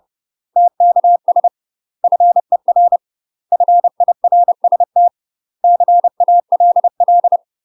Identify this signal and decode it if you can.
Morse code, heard as a keyed tone, about 30 wpm, 700 Hz. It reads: TKS FER FIRST CALL
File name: UK MASTER QSOs TRAINING / 30WPM / UK MASTER QSOs 0015